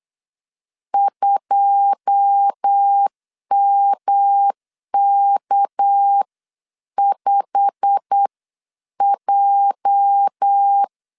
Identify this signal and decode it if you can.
Morse code, heard as a keyed tone, about 8 wpm, spaced as for 7 wpm, 800 Hz. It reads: 2MK5J
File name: 8wpm